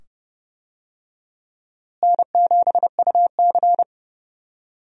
Morse code, heard as a keyed tone, about 30 wpm, 700 Hz. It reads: N7UC
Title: USA Call Signs - 4 Characters - Repeated 2x - 30wpm